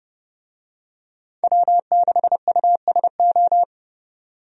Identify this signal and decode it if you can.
Morse code, heard as a keyed tone, about 30 wpm, 700 Hz. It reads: W6USO
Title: USA Call Signs - 5 Characters - 30wpm